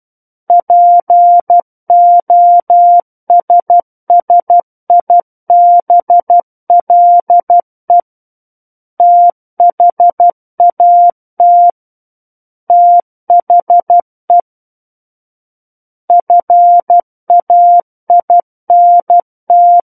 Morse code, heard as a keyed tone, about 12 wpm, 700 Hz. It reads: POSSIBLE THAT THE FAINT
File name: Book One / 03-Chapter 3 - 12 WPM